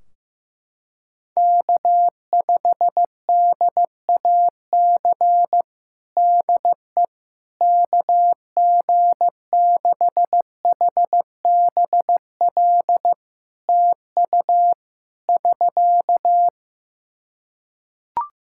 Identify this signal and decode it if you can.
Morse code, heard as a keyed tone, about 15 wpm, 700 Hz. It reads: K5DAC DE KG6HBL TU <SK>
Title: K1USN SST QSO Practice 15wpm